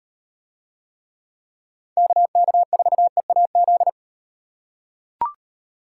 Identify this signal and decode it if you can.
Morse code, heard as a keyed tone, about 38 wpm, 700 Hz. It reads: KK4EAZ